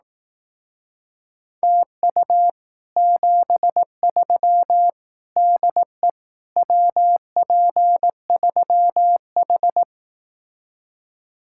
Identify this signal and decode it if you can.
Morse code, heard as a keyed tone, about 18 wpm, 700 Hz. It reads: TU 73 DE WP3H
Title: Speed-Racing Common QSO Phrases Encore - 15wpm